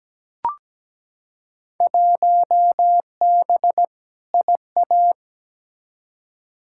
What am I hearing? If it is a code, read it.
Morse code, heard as a keyed tone, about 17 wpm, 700 Hz. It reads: 1B IA
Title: Field Day Exchanges Encore - 17wpm